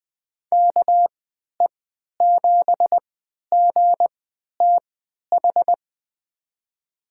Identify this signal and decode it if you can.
Morse code, heard as a keyed tone, about 21 wpm, spaced as for 11 wpm, 700 Hz. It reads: KE7GTH
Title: USA Call Signs - 6 Characters - 3x Character Spacing - 20wpm